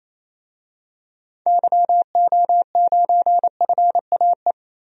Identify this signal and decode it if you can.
Morse code, heard as a keyed tone, about 28 wpm, 700 Hz. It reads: YO9FAE
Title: DXCC-100 Call Signs - 6 Characters - 28wpm